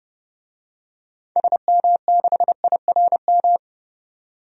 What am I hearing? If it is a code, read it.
Morse code, heard as a keyed tone, about 30 wpm, 700 Hz. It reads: SM6IRM
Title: DXCC-100 Call Signs - 6 Characters - 30wpm